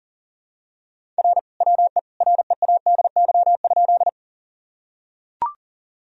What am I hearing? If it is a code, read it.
Morse code, heard as a keyed tone, about 40 wpm, 700 Hz. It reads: R WE READY?
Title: Sentences from Top 400 Words and Abbreviations 40wpm